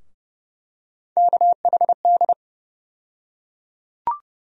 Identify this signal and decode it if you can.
Morse code, heard as a keyed tone, about 30 wpm, 700 Hz. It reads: KHD
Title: USA Call Sign Postfixes - 3 Letters 30wpm